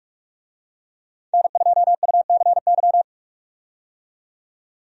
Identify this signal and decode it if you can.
Morse code, heard as a keyed tone, about 46 wpm, 700 Hz. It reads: N2UCY
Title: USA Call Signs - 5 Characters - 45wpm